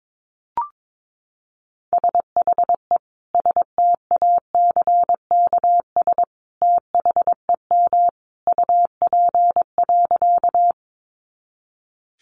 Morse code, heard as a keyed tone, about 22 wpm, 700 Hz. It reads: SHE STACKS THEM UP.
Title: Easy Phrases 22wpm